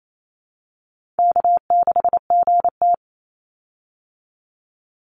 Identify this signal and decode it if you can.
Morse code, heard as a keyed tone, about 28 wpm, 700 Hz. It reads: K6GT
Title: Easy Call Signs Repeated 2x - 28wpm